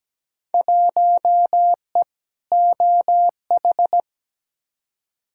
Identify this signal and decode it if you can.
Morse code, heard as a keyed tone, about 17 wpm, 700 Hz. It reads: 1E OH
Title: Field Day Exchanges - 17wpm